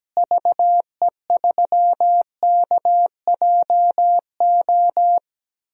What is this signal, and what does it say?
Morse code, heard as a keyed tone, about 17 wpm, 700 Hz. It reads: VE3KJO
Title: DXCC-100 Call Signs - 6 Characters - Repeated 2x - 17wpm